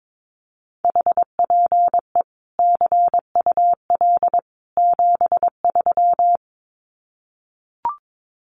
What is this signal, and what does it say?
Morse code, heard as a keyed tone, about 22 wpm, 700 Hz. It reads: HPE CUL 73
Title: Common QSO Phrases 22wpm